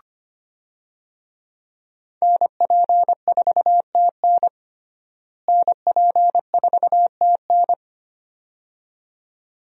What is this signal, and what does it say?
Morse code, heard as a keyed tone, about 25 wpm, 700 Hz. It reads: NP4TN NP4TN
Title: USA Call Signs - 5 Characters - Repeated 2x - 25wpm